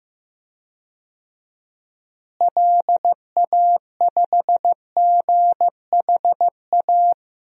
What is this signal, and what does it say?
Morse code, heard as a keyed tone, about 15 wpm, 700 Hz. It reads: LA5GHA
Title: DXCC-100 Call Signs - 6 Characters - Repeated 3x - 15wpm